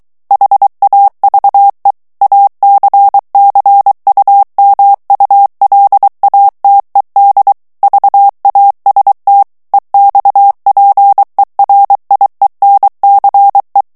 Morse code, heard as a keyed tone, about 23 wpm, 800 Hz. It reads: HAVE ACCUMULATED VAST EXPERIENCE
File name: morse mp3 / ANGLAIS texte clair 2 vitesse 22 mots minute